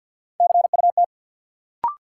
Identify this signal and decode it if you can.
Morse code, heard as a keyed tone, about 49 wpm, 700 Hz. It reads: CUT
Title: Sentences from Top 300 Words - New Word Drill - 50wpm